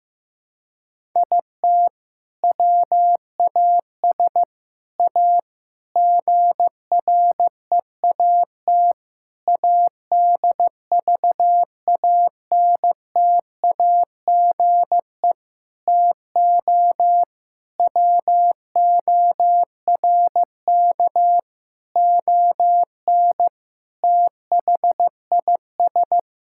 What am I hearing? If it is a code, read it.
Morse code, heard as a keyed tone, about 15 wpm, 700 Hz. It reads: IT WAS A GREAT ADVANTAGE TO WORK ON THIS
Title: Sentences from Top 1000 Words - Review Part 6 - 15wpm